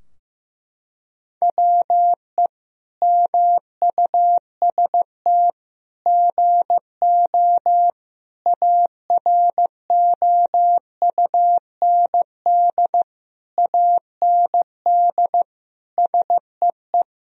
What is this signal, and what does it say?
Morse code, heard as a keyed tone, about 15 wpm, 700 Hz. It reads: WE MUST GO AROUND AND SEE
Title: Sentences from Top 1000 Words - Review Part 3 - 15wpm